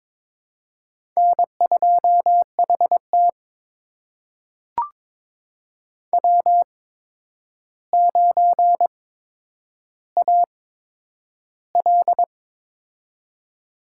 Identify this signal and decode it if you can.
Morse code, heard as a keyed tone, about 22 wpm, spaced as for 9 wpm, 700 Hz. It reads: N2HT W9AL
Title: USA Call Signs - 4 Characters - 8x Character Spacing - 22wpm